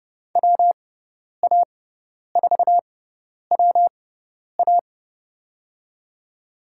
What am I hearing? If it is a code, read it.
Morse code, heard as a keyed tone, about 30 wpm, spaced as for 10 wpm, 700 Hz. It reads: WA4WA
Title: USA Call Signs - 6x Character Spacing - 30wpm